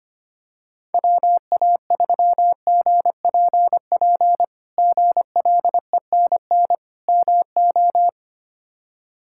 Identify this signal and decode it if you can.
Morse code, heard as a keyed tone, about 25 wpm, 700 Hz. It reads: WA3GPP GLENN MO
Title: K1USN SST QSO Practice - 25wpm